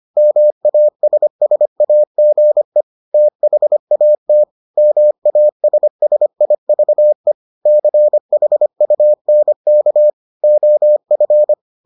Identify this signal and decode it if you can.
Morse code, heard as a keyed tone, about 25 wpm, 600 Hz. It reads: MASSAGE THAT MASSIVE CHUNK OF